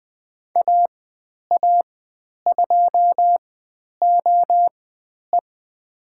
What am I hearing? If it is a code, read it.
Morse code, heard as a keyed tone, about 20 wpm, spaced as for 10 wpm, 700 Hz. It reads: AA2OE
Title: 102 Lesson 17g - Letter u - Call Signs 20wpm